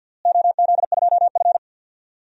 Morse code, heard as a keyed tone, about 53 wpm, 700 Hz. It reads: OZ2F
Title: DXCC-100 Call Signs - 4 Characters - Repeated 3x - 50wpm